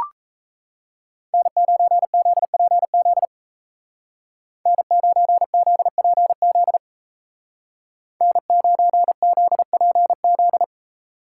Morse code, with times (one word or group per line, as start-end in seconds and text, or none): 1.3-3.3 N9ZPZ
4.7-6.8 N9ZPZ
8.2-10.6 N9ZPZ